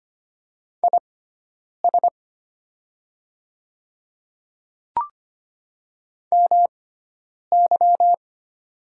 Morse code, heard as a keyed tone, about 25 wpm, spaced as for 9 wpm, 700 Hz. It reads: IS MY
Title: 6x Character Spacing - Top 200 Words - 25wpm